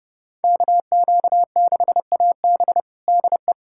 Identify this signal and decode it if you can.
Morse code, heard as a keyed tone, about 30 wpm, 700 Hz. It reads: KQ6AB DE
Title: USA Callsign DE Callsign - 5 Characters - 30wpm